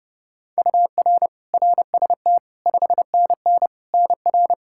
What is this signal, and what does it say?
Morse code, heard as a keyed tone, about 30 wpm, 700 Hz. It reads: UR RST 5NN NR
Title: Practice QSO - Exchange 1 - Beginning - 30wpm